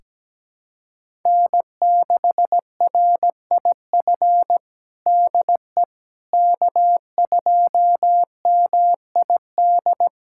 Morse code, heard as a keyed tone, about 17 wpm, 700 Hz. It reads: N6RIF DE K2MID
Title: USA Callsign DE Callsign - 5 Characters - 17wpm